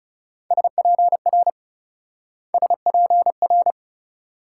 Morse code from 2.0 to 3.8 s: SPR

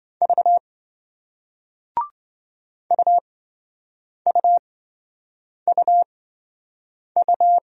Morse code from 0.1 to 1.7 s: V

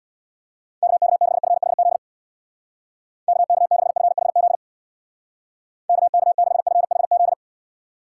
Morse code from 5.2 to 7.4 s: KK6FHZ

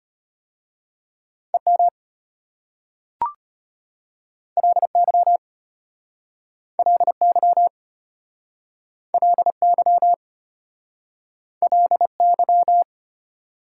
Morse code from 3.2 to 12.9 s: LY LY LY LY

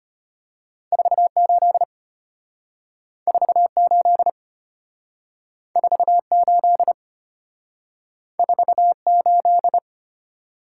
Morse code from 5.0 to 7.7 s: 48

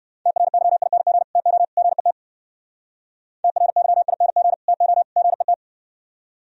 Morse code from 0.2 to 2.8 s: TRYING TO GET

